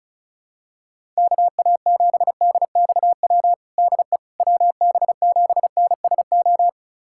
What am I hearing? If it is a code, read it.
Morse code, heard as a keyed tone, about 36 wpm, 700 Hz. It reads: KA7DXW DE WB7NSO
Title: USA Callsign DE Callsign 35wpm